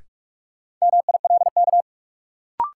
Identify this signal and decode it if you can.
Morse code, heard as a keyed tone, about 46 wpm, 700 Hz. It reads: MILK
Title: Sentences from Top 1500 Words - New Word Drill - 45wpm